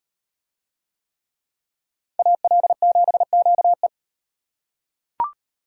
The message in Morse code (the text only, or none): AL7QE